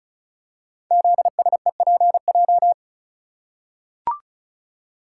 Morse code, heard as a keyed tone, about 34 wpm, 700 Hz. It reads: ZSEPJ
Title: Code Groups - 5 Letters - 35wpm